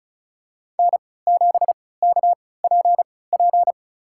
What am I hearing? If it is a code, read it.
Morse code, heard as a keyed tone, about 36 wpm, spaced as for 20 wpm, 700 Hz. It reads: N7KPP
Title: USA Call Signs - 5 Characters - 3x Character Spacing - 35wpm